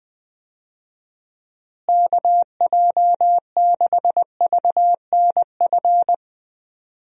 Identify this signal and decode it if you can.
Morse code, heard as a keyed tone, about 20 wpm, 700 Hz. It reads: KJ6VNF
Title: USA Call Signs - 6 Characters - Repeated 3x - 20wpm